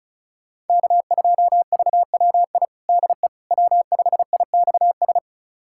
Morse code, heard as a keyed tone, about 36 wpm, 700 Hz. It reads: K2VWI DE W5IXS
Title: USA Callsign DE Callsign - 5 Characters - 35wpm